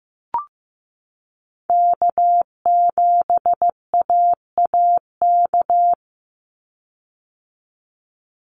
Morse code, heard as a keyed tone, about 15 wpm, 700 Hz. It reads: K7AAK
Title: USA Call Signs 15wpm